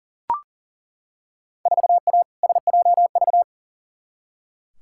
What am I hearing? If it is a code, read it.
Morse code, heard as a keyed tone, about 37 wpm, 700 Hz. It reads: 4A SJV
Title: Field Day Exchanges Encore - 40wpm